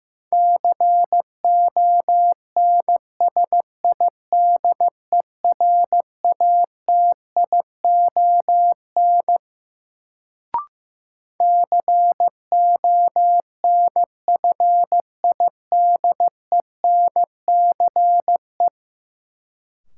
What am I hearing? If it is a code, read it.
Morse code, heard as a keyed tone, about 15 wpm, 700 Hz. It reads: CONSIDERATION CONFIDENCE
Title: Sentences from Top 1500 Words - New Word Drill Alt - 15wpm